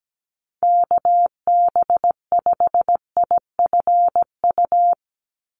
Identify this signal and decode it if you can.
Morse code, heard as a keyed tone, about 17 wpm, 700 Hz. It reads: KB5IFU
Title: USA Call Signs 17wpm